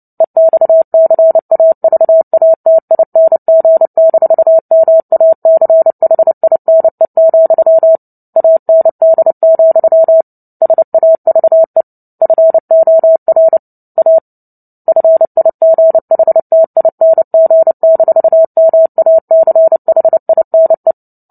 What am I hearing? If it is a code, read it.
Morse code, heard as a keyed tone, about 30 wpm, 650 Hz. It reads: EXCAVATING-MACHINE, AND, SAVE FOR A FIGHTING-MACHINE
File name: Book Two / 03-Chapter 3 - 29 WPM